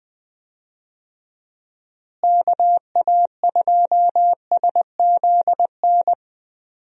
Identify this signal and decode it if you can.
Morse code, heard as a keyed tone, about 21 wpm, 700 Hz. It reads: KA2SZN
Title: USA Call Signs - 6 Characters - Repeated 3x - 20wpm